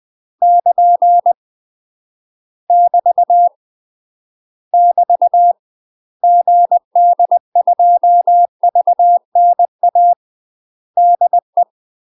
Morse code, heard as a keyed tone, about 20 wpm, 700 Hz. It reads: <KN> <BT> <BT> GD2VNA DE
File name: UK MASTER QSOs TRAINING / 20WPM / UK MASTER QSOs 0038